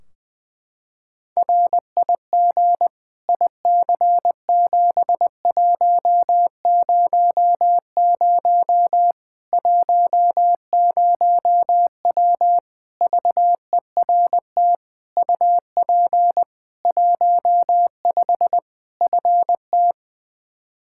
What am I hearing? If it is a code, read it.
Morse code, heard as a keyed tone, about 20 wpm, 700 Hz. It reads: RIG IC7100 10W VERT UP 15 FT
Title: Practice QSO - Exchange 2 - Info - 20wpm